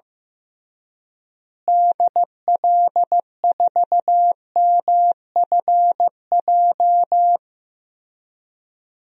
Morse code, heard as a keyed tone, about 15 wpm, 700 Hz. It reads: DL4MFJ
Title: DXCC-100 Call Signs - 6 Characters - Repeated 3x - 15wpm